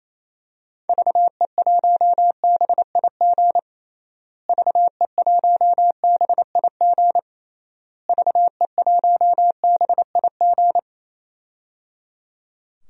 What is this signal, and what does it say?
Morse code, heard as a keyed tone, about 27 wpm, 700 Hz. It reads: VE1BIG VE1BIG VE1BIG
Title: DXCC-100 Call Signs - 6 Characters - Repeated 3x - 28wpm